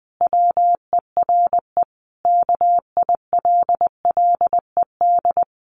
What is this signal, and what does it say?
Morse code, heard as a keyed tone, about 20 wpm, 700 Hz. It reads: WERE KILLED
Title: A Sound Of Thunder 20wpm